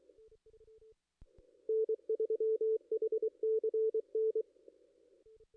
Morse code, heard as a keyed tone, about 24 wpm, 440 Hz. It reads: N3HCN
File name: contests / ss-cw-k5tr-op-n5rz -2300